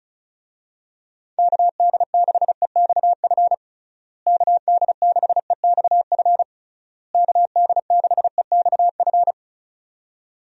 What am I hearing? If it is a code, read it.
Morse code, heard as a keyed tone, about 36 wpm, 700 Hz. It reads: KD6EXF KD6EXF KD6EXF
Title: USA Call Signs - 6 Characters - Repeated 3x - 35wpm